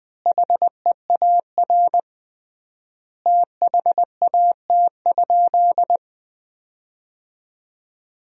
Morse code, heard as a keyed tone, about 20 wpm, 700 Hz. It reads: HEAR THAT?
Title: 3x Word-Spacing - Sentences from Top 300 Words - 20wpm